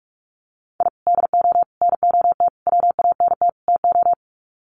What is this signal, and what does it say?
Morse code, heard as a keyed tone, about 44 wpm, 700 Hz. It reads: I DO NOT WANT TO
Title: Sets of 5 Words 45wpm